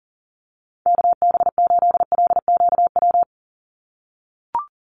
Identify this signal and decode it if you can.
Morse code, heard as a keyed tone, about 40 wpm, 700 Hz. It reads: KB8LQW
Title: USA Call Signs 40wpm